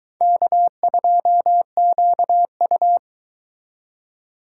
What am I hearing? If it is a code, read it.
Morse code, heard as a keyed tone, about 22 wpm, 700 Hz. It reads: K2QU